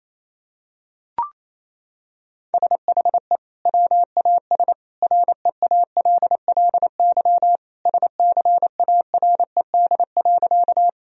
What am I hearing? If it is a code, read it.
Morse code, heard as a keyed tone, about 28 wpm, 700 Hz. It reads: SHE WAS REALLY SCARED.
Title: Sentences from Top 1000 Words - Review Part 4 - 28wpm